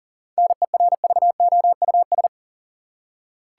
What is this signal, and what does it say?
Morse code, heard as a keyed tone, about 40 wpm, 700 Hz. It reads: NERVOUS
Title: Seven Letter Words - 40wpm